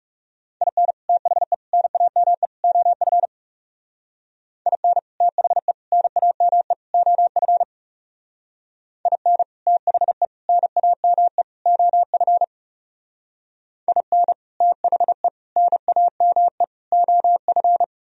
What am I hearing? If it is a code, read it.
Morse code, heard as a keyed tone, about 36 wpm, 700 Hz. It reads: IN THE NAME OF IN THE NAME OF IN THE NAME OF IN THE NAME OF
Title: Speed-Racing - Sets of 4 Words 30wpm